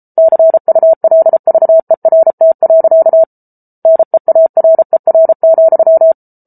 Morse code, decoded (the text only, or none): CULVERT. NEARER,